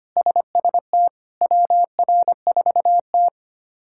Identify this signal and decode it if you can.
Morse code, heard as a keyed tone, about 25 wpm, 700 Hz. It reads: SST WR4T
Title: K1USN SST QSO Practice - 25wpm